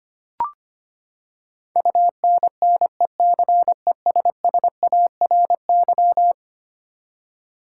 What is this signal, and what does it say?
Morse code, heard as a keyed tone, about 25 wpm, 700 Hz. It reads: UNNECESSARY
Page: morse-code-ninja-practice-25wpm